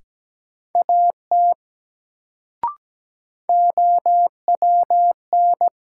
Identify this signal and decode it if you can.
Morse code, heard as a keyed tone, about 17 wpm, 700 Hz. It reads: AT OWN
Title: Sets of 2 Words - New Word Drill - 17wpm